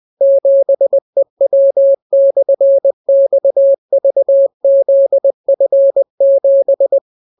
Morse code, heard as a keyed tone, about 20 wpm, 550 Hz. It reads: 7EW/XVZF7